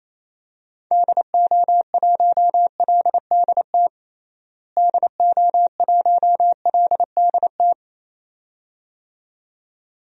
Morse code, heard as a keyed tone, about 28 wpm, 700 Hz. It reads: DO1LDT DO1LDT
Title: DXCC-100 Call Signs - 6 Characters - Repeated 2x - 28wpm